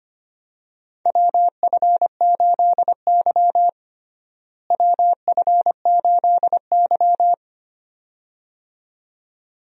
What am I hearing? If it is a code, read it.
Morse code, heard as a keyed tone, about 25 wpm, 700 Hz. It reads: WF8Y WF8Y
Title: USA Call Signs - 4 Characters - Repeated 2x - 25wpm